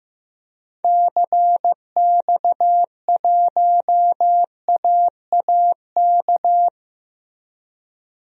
Morse code, heard as a keyed tone, about 15 wpm, 700 Hz. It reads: CX1AAK
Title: DXCC-100 Call Signs - 6 Characters - Repeated 2x - 15wpm